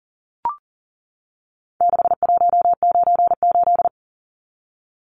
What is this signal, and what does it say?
Morse code, heard as a keyed tone, about 40 wpm, 700 Hz. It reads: 6198
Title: Numbers - 4-Digit 40wpm